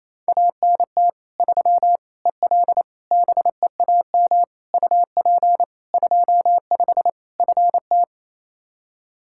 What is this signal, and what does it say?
Morse code, heard as a keyed tone, about 28 wpm, 700 Hz. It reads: ANT 3 EL BEAM UP 25 FT
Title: Practice QSO - Exchange 2 - Info - 28wpm